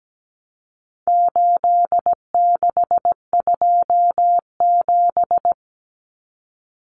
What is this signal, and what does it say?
Morse code, heard as a keyed tone, about 17 wpm, 700 Hz. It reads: 8627
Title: Numbers - 4-Digit 17wpm